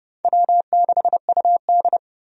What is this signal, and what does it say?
Morse code, heard as a keyed tone, about 32 wpm, 700 Hz. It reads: W6UD
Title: USA Call Signs - 4 Characters - 30wpm